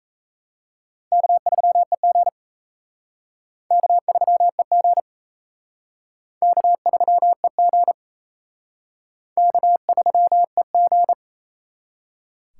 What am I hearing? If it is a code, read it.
Morse code, heard as a keyed tone, about 36 wpm, 700 Hz. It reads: K3EG K3EG K3EG K3EG